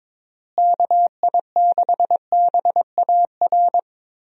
Morse code, heard as a keyed tone, about 22 wpm, 700 Hz. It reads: KI6BAR
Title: USA Call Signs - 6 Characters - Repeated 2x - 22wpm